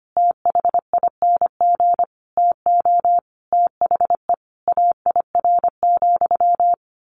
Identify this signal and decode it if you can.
Morse code, heard as a keyed tone, about 25 wpm, 700 Hz. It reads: THING TO THE AIR,
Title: A Sound Of Thunder 25wpm